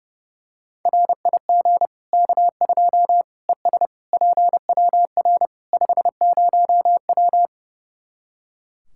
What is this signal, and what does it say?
Morse code, heard as a keyed tone, about 30 wpm, 700 Hz. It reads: RIG K2 ES PWR 50W
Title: Practice QSO - Exchange 2 - Info - 30wpm